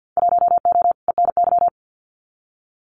Morse code, heard as a keyed tone, about 50 wpm, 700 Hz. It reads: 1O ENY
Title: Winter Field Day Exchanges 50wpm